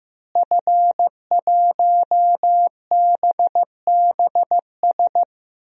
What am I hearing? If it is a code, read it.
Morse code, heard as a keyed tone, about 16 wpm, 700 Hz. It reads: F1BBS
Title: DXCC-100 Call Signs - 5 Characters - Repeated 3x - 15wpm